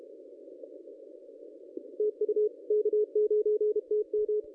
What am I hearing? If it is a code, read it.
Morse code, heard as a keyed tone, about 30 wpm, 440 Hz. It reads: TU K9TM
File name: contests / ss-cw-k5tr-op-n5rz -1100